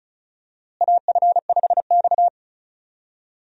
Morse code, heard as a keyed tone, about 36 wpm, 700 Hz. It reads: AF5X
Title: USA Call Signs - 4 Characters - Repeated 3x - 35wpm